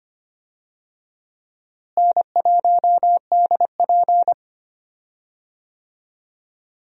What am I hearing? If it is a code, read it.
Morse code, heard as a keyed tone, about 25 wpm, 700 Hz. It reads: N1DP
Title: USA Call Signs - 4 Characters - Repeated 2x - 25wpm